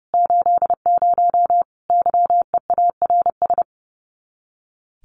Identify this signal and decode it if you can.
Morse code, heard as a keyed tone, about 30 wpm, 700 Hz. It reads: 80 YEARS
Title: Common QSO Phrases 30wpm